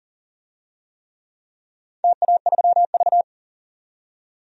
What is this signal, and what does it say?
Morse code, heard as a keyed tone, about 38 wpm, 700 Hz. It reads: TA3V
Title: DXCC-100 Call Signs - 4 Characters - 40wpm